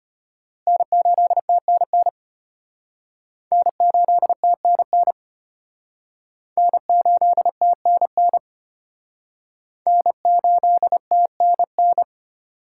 Morse code, heard as a keyed tone, about 32 wpm, 700 Hz. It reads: N8TNN N8TNN N8TNN N8TNN